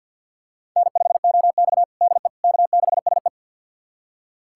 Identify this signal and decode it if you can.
Morse code, heard as a keyed tone, about 52 wpm, 700 Hz. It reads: N5OX DE K6SE
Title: USA Callsign DE Callsign - 4 Characters - 50wpm